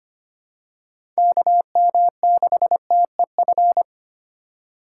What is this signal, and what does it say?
Morse code, heard as a keyed tone, about 25 wpm, 700 Hz. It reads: KM6TEF
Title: USA Call Signs - 6 Characters - 4x Character Spacing - 25wpm